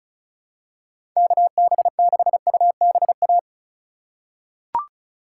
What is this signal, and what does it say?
Morse code, heard as a keyed tone, about 34 wpm, 700 Hz. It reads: KB6UBA
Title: USA Call Signs - 6 Characters - 4x Character Spacing - 35wpm